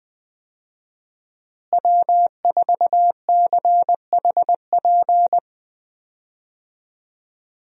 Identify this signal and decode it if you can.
Morse code, heard as a keyed tone, about 20 wpm, 700 Hz. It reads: W4CHP
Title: USA Call Signs - 5 Characters - Repeated 2x - 20wpm